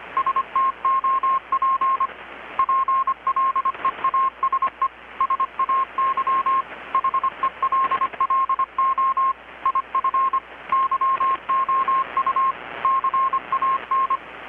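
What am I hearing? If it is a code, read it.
Morse code, heard as a keyed tone, about 25 wpm, 1070 Hz. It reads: STOP PLEASE SAY HELLO IF YOU CAN
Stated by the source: Morse from KPH's Night of Nights XX maritime radiotelegraphy event, received in Chiapas, Mexico about 3,800 km away on a Kenwood R-600 with a grounded 10 m long-wire antenna.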